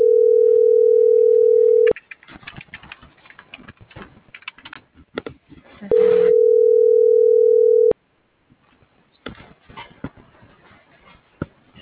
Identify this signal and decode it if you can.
Ringback tone (North America), 440 + 480 Hz.